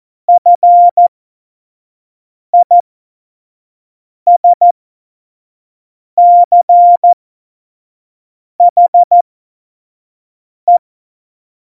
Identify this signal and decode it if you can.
Morse code, heard as a keyed tone, about 14 wpm, spaced as for 5 wpm, 700 Hz. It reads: FISCHE
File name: Morsekurs / Witze / Witz 05. September 17, Tempo 25